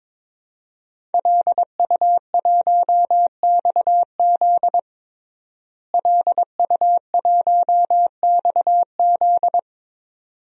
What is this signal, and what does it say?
Morse code, heard as a keyed tone, about 21 wpm, 700 Hz. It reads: LU1XZ LU1XZ
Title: DXCC-100 Call Signs - 5 Characters - Repeated 2x - 22wpm